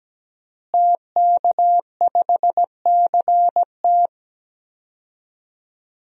Morse code, heard as a keyed tone, about 17 wpm, 700 Hz. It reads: TK5CT
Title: DXCC-100 Call Signs - 5 Characters - Repeated 2x - 17wpm